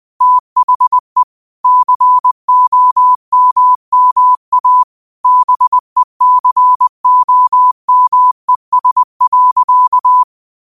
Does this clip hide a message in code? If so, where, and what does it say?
Yes: Morse code from 0.2 to 10.2 s, heard as a keyed tone, about 20 wpm, 1000 Hz. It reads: THE COMMA BECOMES.